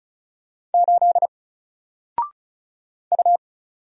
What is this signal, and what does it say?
Morse code, heard as a keyed tone, about 36 wpm, 700 Hz. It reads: 8 U
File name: Single-Letter-Number-35wpm